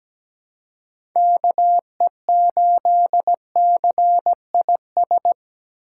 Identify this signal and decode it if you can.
Morse code, heard as a keyed tone, about 17 wpm, 700 Hz. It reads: KE8CIS
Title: USA Call Signs - 6 Characters - 4x Character Spacing - 17wpm